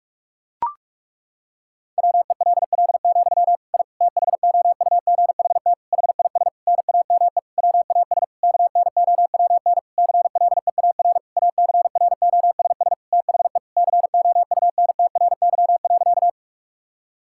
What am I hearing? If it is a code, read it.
Morse code, heard as a keyed tone, about 41 wpm, 700 Hz. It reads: WELL, I THOUGHT HIS NAME WAS KNOWN CLEAR ACROSS THE COUNTRY.
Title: Sentences from Top 1000 Words - Review Part 5 - 45wpm